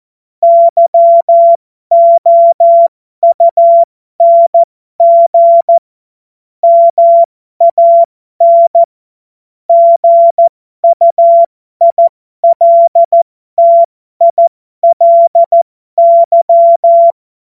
Morse code, heard as a keyed tone, about 14 wpm, spaced as for 12 wpm, 680 Hz. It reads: YOUNG MAN GUILTILY